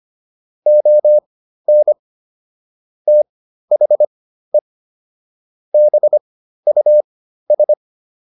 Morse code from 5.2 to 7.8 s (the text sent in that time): BUS